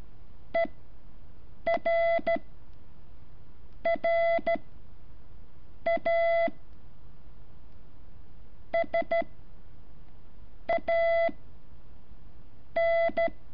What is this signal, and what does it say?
Morse code, heard as a keyed tone, about 13 wpm, 670 Hz. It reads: E R R A S A N
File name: curso / ejercicio 11